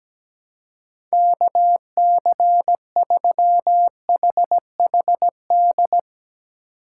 Morse code, heard as a keyed tone, about 17 wpm, 700 Hz. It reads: KC3HHD
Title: USA Call Signs - 6 Characters - 3x Character Spacing - 17wpm